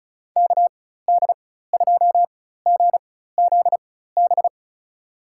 Morse code, heard as a keyed tone, about 36 wpm, spaced as for 16 wpm, 690 Hz. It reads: KD2GZB
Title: USA Call Signs - 6 Characters - 4x Character Spacing - 35wpm